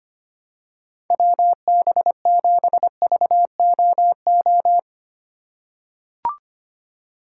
Morse code, heard as a keyed tone, about 25 wpm, 700 Hz. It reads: WB7VOO
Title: USA Call Signs - 4x Character Spacing - 25wpm